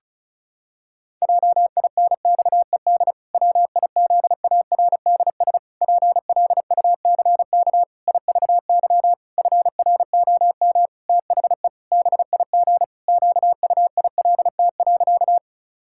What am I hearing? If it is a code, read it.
Morse code, heard as a keyed tone, about 36 wpm, 700 Hz. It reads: JINXED WIZARDS PLUCK IVY FROM THE BIG QUILT.
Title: Pangram Challenge v2 - 35wpm